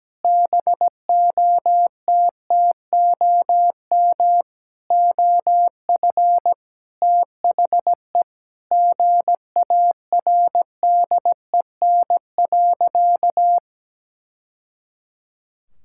Morse code, heard as a keyed tone, about 17 wpm, 700 Hz. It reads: BOTTOM OF THE GARDEN.
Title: Sentences from Top 1000 Words - Review Part 6 - 17wpm